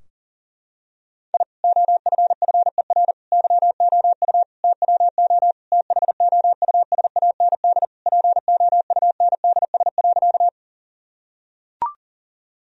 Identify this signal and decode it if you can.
Morse code, heard as a keyed tone, about 40 wpm, 700 Hz. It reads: I OFFER YOU TWO THOUSAND POUNDS.
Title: Sentences from Top 1000 Words - Review Part 4 - 40wpm